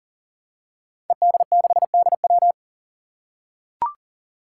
Morse code, heard as a keyed tone, about 40 wpm, 700 Hz. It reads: ED6DW
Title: Five Challenging Letter-Number Combinations 40wpm